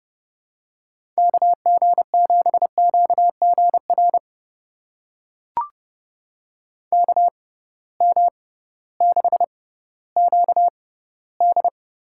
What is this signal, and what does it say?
Morse code, heard as a keyed tone, about 32 wpm, spaced as for 17 wpm, 700 Hz. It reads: KG7QGR KM6QD
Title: USA Call Signs - 6 Characters - 6x Character Spacing - 30wpm